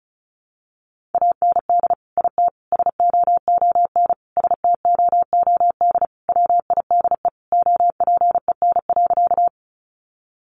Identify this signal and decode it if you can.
Morse code, heard as a keyed tone, about 34 wpm, 700 Hz. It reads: AND IT SOON STOOD WIDE OPEN.
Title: Sentences from Top 900 Words 35wpm